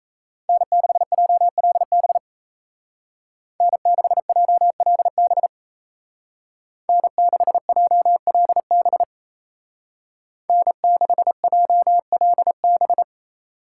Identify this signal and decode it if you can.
Morse code, heard as a keyed tone, about 36 wpm, 700 Hz. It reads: N6JLB N6JLB N6JLB N6JLB